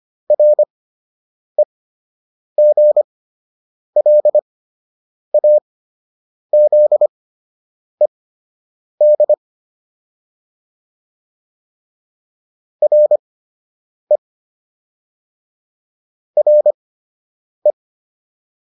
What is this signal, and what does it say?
Morse code, heard as a keyed tone, about 25 wpm, spaced as for 8 wpm, 600 Hz. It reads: REGLAZED RE RE